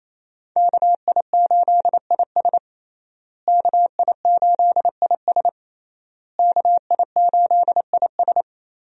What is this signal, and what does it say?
Morse code, heard as a keyed tone, about 27 wpm, 700 Hz. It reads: KI8IS KI8IS KI8IS
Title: USA Call Signs - 5 Characters - Repeated 3x - 28wpm